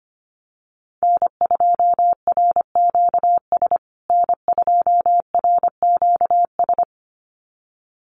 Morse code, heard as a keyed tone, about 25 wpm, 700 Hz. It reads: N2RQS N2RQS
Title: USA Call Signs Repeated 2x 25wpm